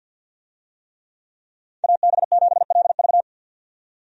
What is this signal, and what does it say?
Morse code, heard as a keyed tone, about 47 wpm, 700 Hz. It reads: AB7LV
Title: USA Call Signs - 5 Characters - 50wpm